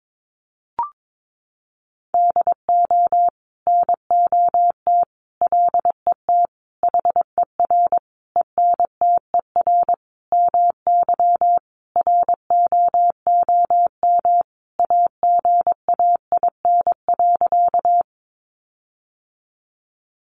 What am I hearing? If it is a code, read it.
Morse code, heard as a keyed tone, about 22 wpm, 700 Hz. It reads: DO NOT LET HER ENTER MY ROOM AGAIN.
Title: Sentences from Top 1300 Words 22wpm